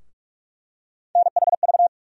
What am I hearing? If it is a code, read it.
Morse code, heard as a keyed tone, about 42 wpm, 700 Hz. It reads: NHV
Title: USA Call Sign Postfixes - 3 Letters 45wpm